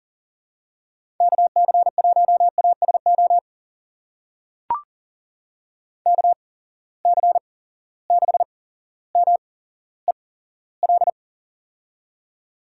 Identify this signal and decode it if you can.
Morse code, heard as a keyed tone, about 38 wpm, spaced as for 17 wpm, 700 Hz. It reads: KC1ASO KC6MEL
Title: USA Call Signs - 6 Characters - 8x Character Spacing - 40wpm